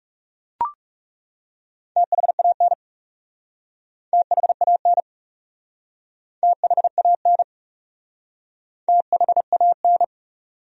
Morse code, heard as a keyed tone, about 37 wpm, 700 Hz. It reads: THAN THAN THAN THAN
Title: Speed-Racing Top 200 Words 30wpm